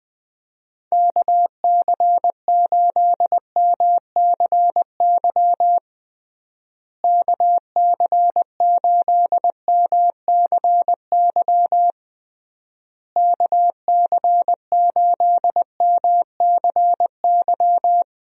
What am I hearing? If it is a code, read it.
Morse code, heard as a keyed tone, about 20 wpm, 700 Hz. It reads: KC8MCY KC8MCY KC8MCY
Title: USA Call Signs - 6 Characters - Repeated 3x - 20wpm